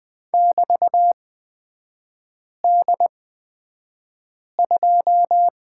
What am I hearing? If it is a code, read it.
Morse code, heard as a keyed tone, about 20 wpm, spaced as for 5 wpm, 700 Hz. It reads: <BT>D2
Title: 243 Lesson 39e - Letter z - Three Characters 20wpm